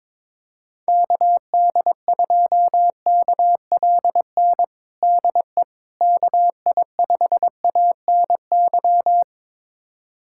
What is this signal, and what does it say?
Morse code, heard as a keyed tone, about 22 wpm, 700 Hz. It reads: KD2KLN DE KI5ANY
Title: USA Callsign DE Callsign - 6 Characters - 22wpm